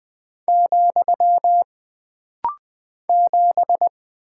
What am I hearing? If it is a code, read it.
Morse code, heard as a keyed tone, about 20 wpm, 700 Hz. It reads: , 7
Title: Single Character (Letter:Number:Punctuation:Prosign) - Rapid-Fire - 20wpm